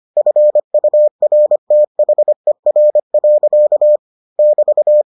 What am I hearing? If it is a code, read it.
Morse code, heard as a keyed tone, about 25 wpm, 600 Hz. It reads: FURTHER. <BT>